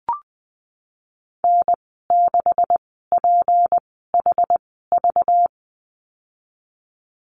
Morse code, heard as a keyed tone, about 20 wpm, spaced as for 15 wpm, 700 Hz. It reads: N6PHV
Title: 207 Lesson 33g - Letter j - Call Signs 20wpm